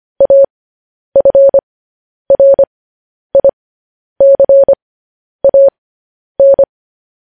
Morse code, heard as a keyed tone, about 25 wpm, spaced as for 10 wpm, 550 Hz. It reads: AFRICAN
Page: news-headlines-in-morse-code-at-10-wpm